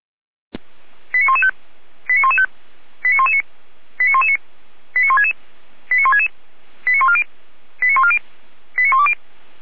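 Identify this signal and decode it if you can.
Five-tone selective call (ZVEI), heard as quick runs of five tones, tones of about 70 ms. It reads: 89115 89115 89119 89119 89166 89166 89140 89140 89120
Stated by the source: ZVEI five-tone call 89050 received on SDR.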